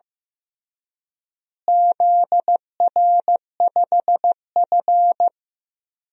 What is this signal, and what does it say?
Morse code, heard as a keyed tone, about 15 wpm, 700 Hz. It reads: ZR5F
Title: DXCC-100 Call Signs - 4 Characters - Repeated 3x - 15wpm